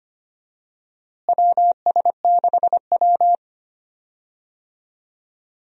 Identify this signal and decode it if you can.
Morse code, heard as a keyed tone, about 25 wpm, 700 Hz. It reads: WS6W
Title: USA Call Signs - 4 Characters - Repeated 3x - 25wpm